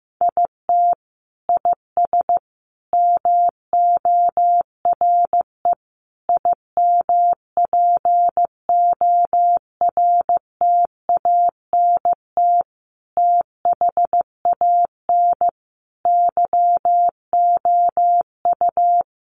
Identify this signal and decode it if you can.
Morse code, heard as a keyed tone, about 15 wpm, 700 Hz. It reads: IT IS MORE IMPORTANT THAN YOU
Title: Sentences from Top 600 Words 15wpm